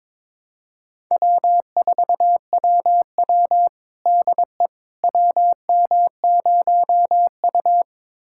Morse code, heard as a keyed tone, about 22 wpm, 700 Hz. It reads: W4WW DE WM0U
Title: USA Callsign DE Callsign - 4 Characters - 22wpm